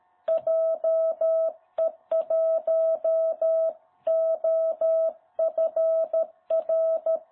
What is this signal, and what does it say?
Morse code, heard as a keyed tone, about 13 wpm, 660 Hz. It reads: JE1OFR
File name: jh8cbh / 2021cw